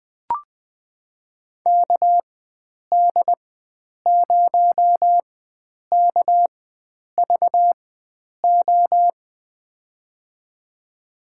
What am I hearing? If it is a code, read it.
Morse code, heard as a keyed tone, about 21 wpm, spaced as for 9 wpm, 700 Hz. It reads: KD0KVO
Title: USA Call Signs - 6 Characters - 4x Character Spacing - 20wpm